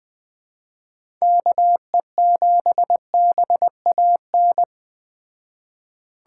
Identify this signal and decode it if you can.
Morse code, heard as a keyed tone, about 19 wpm, 700 Hz. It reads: KE7BAN